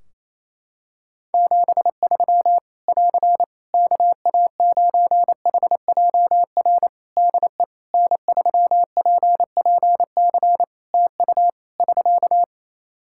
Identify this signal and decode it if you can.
Morse code, heard as a keyed tone, about 28 wpm, 700 Hz. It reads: 73 <AR> KA9HJR DE N3PPC TU <SK>
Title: Practice QSO - Exchange 4 - Ending - 28wpm